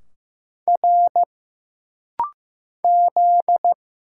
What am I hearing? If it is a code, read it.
Morse code, heard as a keyed tone, about 15 wpm, 700 Hz. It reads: R Z
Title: Single Letters - Rapid-Fire 15wpm